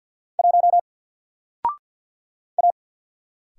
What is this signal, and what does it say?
Morse code, heard as a keyed tone, about 48 wpm, 700 Hz. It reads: 1 A
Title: Single Character (Letter:Number:Punctuation:Prosign) 50wpm